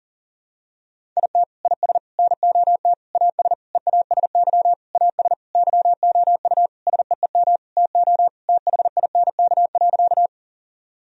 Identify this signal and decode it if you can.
Morse code, heard as a keyed tone, about 40 wpm, 700 Hz. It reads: IT IS NOT AS EASY AS YOU SEEM TO THINK.
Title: Sentences from Top 1000 Words - Review Part 1 - 40wpm